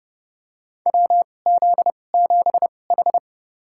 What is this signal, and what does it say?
Morse code, heard as a keyed tone, about 30 wpm, spaced as for 22 wpm, 700 Hz. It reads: WZ7H
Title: USA Call Signs - 2x Character Spacing - 30wpm